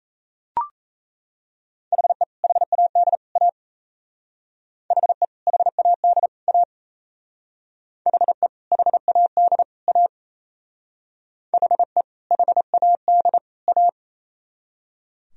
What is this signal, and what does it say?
Morse code, heard as a keyed tone, about 35 wpm, 700 Hz. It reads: HE HAD A HE HAD A HE HAD A HE HAD A